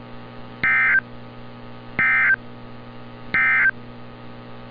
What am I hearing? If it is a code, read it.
Emergency Alert System (SAME) tones, heard as harsh buzzing bursts. It is heard from 0.5 to 3.8 s.
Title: NOAA Weather Radio - Altoona, PA